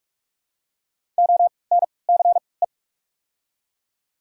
Morse code, heard as a keyed tone, about 46 wpm, spaced as for 26 wpm, 700 Hz. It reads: ONCE
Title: 3x Character Spacing - Top 300 Words - 45wpm